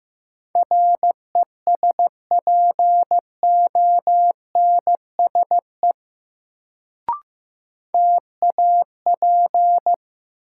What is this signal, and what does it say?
Morse code, heard as a keyed tone, about 15 wpm, 700 Hz. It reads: RESPONSE TAP
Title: Sentences from Top 1400 Words - New Word Drill - 15wpm